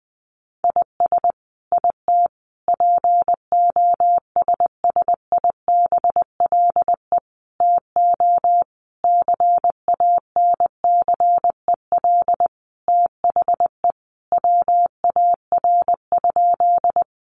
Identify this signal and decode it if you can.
Morse code, heard as a keyed tone, about 20 wpm, 700 Hz. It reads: IS IT POSSIBLE TO CANCEL THE WAR?
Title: Sentences from Top 1200 Words 20wpm